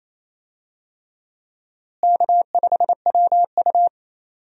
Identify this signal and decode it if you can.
Morse code, heard as a keyed tone, about 28 wpm, 700 Hz. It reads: K5WU
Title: USA Call Signs - 4 Characters - Repeated 3x - 28wpm